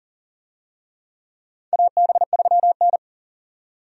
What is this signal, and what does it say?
Morse code, heard as a keyed tone, about 40 wpm, 700 Hz. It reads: AB3N
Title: USA Call Signs - 4 Characters - 40wpm